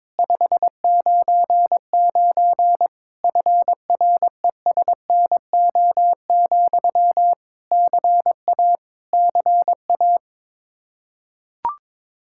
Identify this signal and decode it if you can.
Morse code, heard as a keyed tone, about 21 wpm, 700 Hz. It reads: 599 FRESNO, CA CA
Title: Practice QSO - Exchange 1 - Beginning - 22wpm